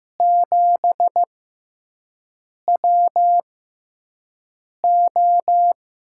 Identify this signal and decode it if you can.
Morse code, heard as a keyed tone, about 15 wpm, spaced as for 5 wpm, 700 Hz. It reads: 7WO
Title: USA Call Signs - 6x Character Spacing - 15wpm